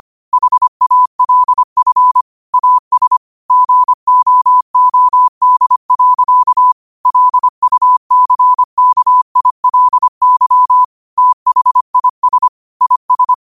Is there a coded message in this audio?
Yes: Morse code from 0.3 to 13.3 s, heard as a keyed tone, about 25 wpm, 1000 Hz. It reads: HALF AS GOOD. LUCKILY THIS IS